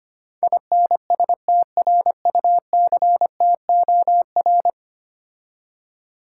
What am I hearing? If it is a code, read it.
Morse code, heard as a keyed tone, about 25 wpm, 700 Hz. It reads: INSTRUCTOR
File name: 10 Letter Words - First 3 Letters Given - 25wpm